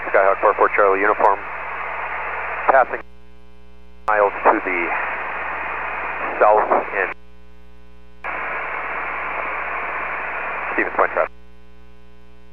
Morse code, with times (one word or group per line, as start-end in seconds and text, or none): none